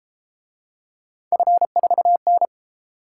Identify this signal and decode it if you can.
Morse code, heard as a keyed tone, about 29 wpm, 700 Hz. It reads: F4N